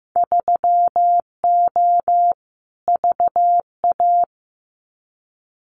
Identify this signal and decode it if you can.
Morse code, heard as a keyed tone, about 15 wpm, 700 Hz. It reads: 3O VA
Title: Winter Field Day Exchanges 15wpm